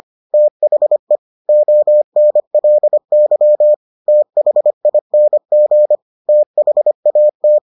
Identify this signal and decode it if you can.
Morse code, heard as a keyed tone, about 25 wpm, 600 Hz. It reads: THE ONLY THING THAT